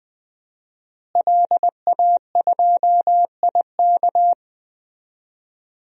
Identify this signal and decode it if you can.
Morse code, heard as a keyed tone, about 21 wpm, 700 Hz. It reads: LA2IK
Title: DXCC-100 Call Signs - 5 Characters - Repeated 2x - 20wpm